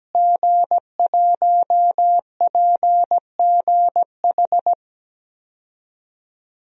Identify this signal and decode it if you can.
Morse code, heard as a keyed tone, about 17 wpm, 700 Hz. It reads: G1PGH
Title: DXCC-100 Call Signs - 5 Characters - Repeated 2x - 17wpm